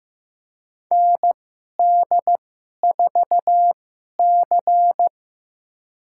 Morse code, heard as a keyed tone, about 15 wpm, spaced as for 11 wpm, 700 Hz. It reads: ND4C
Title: USA Call Signs - 4 Characters - 2x Character Spacing - 15wpm